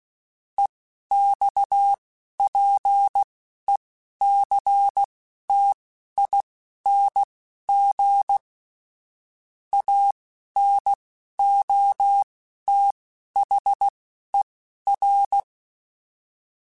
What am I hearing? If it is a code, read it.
Morse code, heard as a keyed tone, about 16 wpm, spaced as for 11 wpm, 800 Hz. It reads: EXPECTING ANOTHER
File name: QSOs / QSOs-mp3s / WA7QOX 1@10